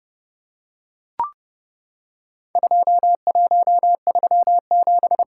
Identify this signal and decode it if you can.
Morse code, heard as a keyed tone, about 30 wpm, 700 Hz. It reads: 2137
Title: Worked All Europe DX Contest - QTC Exchange - 30wpm